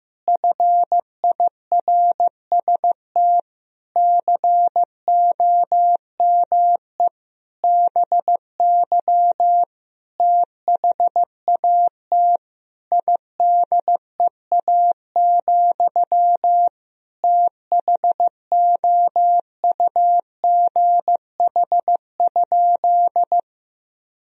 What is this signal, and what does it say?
Morse code, heard as a keyed tone, about 14 wpm, 700 Hz. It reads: FIRST COME BY THAT IDEA, THOUGH?
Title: Sentences from Top 1000 Words - Review Part 2 - 15wpm